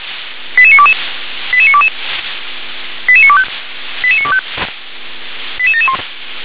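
Five-tone selective call (ZVEI), heard as quick runs of five tones, tones of about 70 ms. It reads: 88922 88922 88925 88925 88771
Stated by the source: Five-tone ZVEI selective call received on an SDR.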